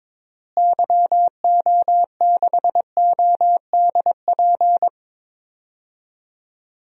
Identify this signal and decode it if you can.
Morse code, heard as a keyed tone, about 22 wpm, 700 Hz. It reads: YO6ODP
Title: DXCC-100 Call Signs - 6 Characters - Repeated 2x - 22wpm